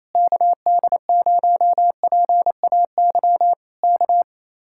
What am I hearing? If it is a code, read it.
Morse code, heard as a keyed tone, about 26 wpm, 700 Hz. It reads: KD0PAY K
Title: Speed-Racing Common QSO Phrases Encore - 28wpm